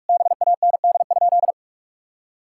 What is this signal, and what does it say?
Morse code, heard as a keyed tone, about 42 wpm, 700 Hz. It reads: BAND?
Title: Sentences from Top 1600 Words 45wpm